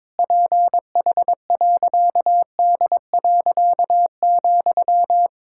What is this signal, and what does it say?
Morse code, heard as a keyed tone, about 22 wpm, 700 Hz. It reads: PH.D.,
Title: Pangram Challenge v2 - 22wpm